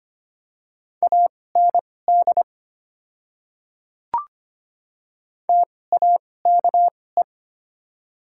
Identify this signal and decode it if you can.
Morse code, heard as a keyed tone, about 25 wpm, spaced as for 18 wpm, 700 Hz. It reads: AND TAKE
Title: 2x Character Spacing - Top 300 Words - 25wpm